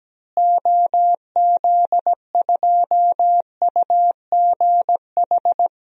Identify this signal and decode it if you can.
Morse code, heard as a keyed tone, about 17 wpm, 700 Hz. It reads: OZ2UGH
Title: DXCC-100 Call Signs - 6 Characters - Repeated 3x - 17wpm